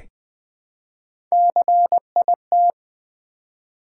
Morse code, heard as a keyed tone, about 20 wpm, 700 Hz. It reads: CIT
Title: Build A Word - 1 Letter At A Time - 4-Letter Words - 20wpm